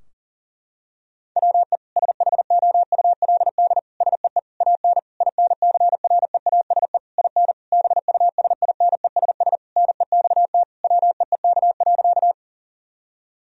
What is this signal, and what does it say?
Morse code, heard as a keyed tone, about 40 wpm, 700 Hz. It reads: WE SHOULD SEE AN INCREASE IN BUSINESS NEXT WEEK.
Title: Sentences from Top 1000 Words - Review Part 4 - 40wpm